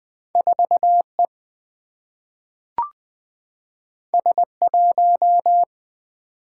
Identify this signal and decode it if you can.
Morse code, heard as a keyed tone, about 20 wpm, 700 Hz. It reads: 4 E S 1
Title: Single-Letter Single-Number 20wpm